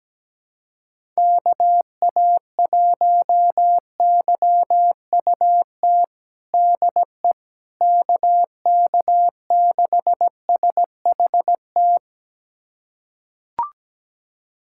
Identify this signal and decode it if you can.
Morse code, heard as a keyed tone, about 17 wpm, 700 Hz. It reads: KA1YUT DE KK6SHT
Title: USA Callsign DE Callsign 17wpm